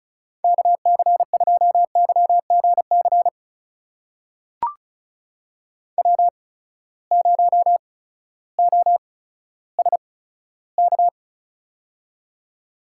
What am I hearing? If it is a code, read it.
Morse code, heard as a keyed tone, about 36 wpm, spaced as for 16 wpm, 700 Hz. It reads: KC2YGC W0OSK
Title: USA Call Signs - 8x Character Spacing - 35wpm